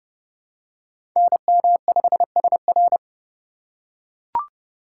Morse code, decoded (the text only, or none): NM5SR